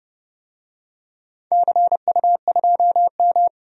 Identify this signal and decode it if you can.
Morse code, heard as a keyed tone, about 30 wpm, 700 Hz. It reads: CU2M
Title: DXCC-100 Call Signs - 4 Characters - 30wpm